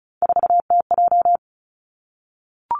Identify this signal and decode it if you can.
Morse code, heard as a keyed tone, about 36 wpm, 700 Hz. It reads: 4TJ
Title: Three Challenging Letter-Number Combinations 35wpm